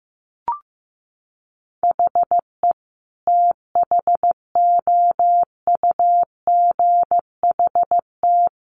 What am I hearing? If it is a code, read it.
Morse code, heard as a keyed tone, about 15 wpm, 700 Hz. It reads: HE THOUGHT
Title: Sentences from Top 1500 Words 15wpm